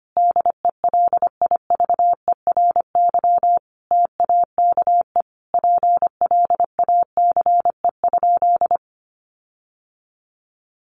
Morse code, heard as a keyed tone, about 25 wpm, 700 Hz. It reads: DELIVERY TAKE PLACE?
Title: Sentences from Top 1500 Words 25wpm